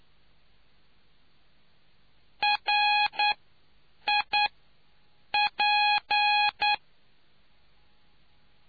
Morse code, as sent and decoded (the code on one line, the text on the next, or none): .-. / .. / .--.
R I P